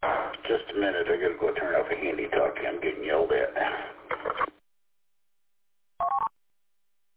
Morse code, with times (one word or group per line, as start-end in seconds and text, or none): none